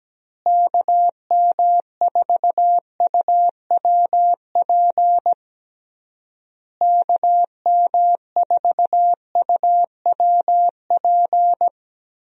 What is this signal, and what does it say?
Morse code, heard as a keyed tone, about 17 wpm, 700 Hz. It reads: KM4UWP KM4UWP
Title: USA Call Signs - 6 Characters - Repeated 3x - 17wpm